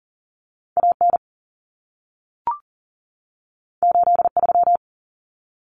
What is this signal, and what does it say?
Morse code, heard as a keyed tone, about 40 wpm, 700 Hz. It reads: AN 83
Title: Common Cut Numbers - 2-Digit 40wpm